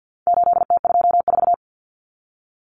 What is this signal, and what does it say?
Morse code, heard as a keyed tone, about 50 wpm, 700 Hz. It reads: 8T24
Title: Common Cut Numbers - 4-Digit 50wpm